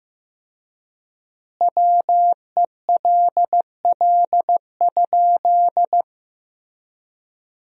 Morse code, heard as a keyed tone, about 15 wpm, 700 Hz. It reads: WELL?
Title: 4x Word-Spacing - Sentences from Top 400 Words - 15wpm